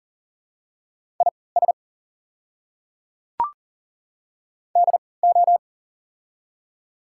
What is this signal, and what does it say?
Morse code, heard as a keyed tone, about 40 wpm, spaced as for 23 wpm, 700 Hz. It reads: IS DO
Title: 3x Character Spacing - Top 200 Words - 40wpm